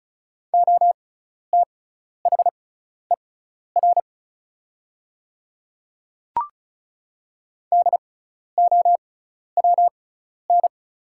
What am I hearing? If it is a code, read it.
Morse code, heard as a keyed tone, about 36 wpm, spaced as for 12 wpm, 700 Hz. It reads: OTHER DOWN
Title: 6x Character Spacing - Top 100 Words - 35wpm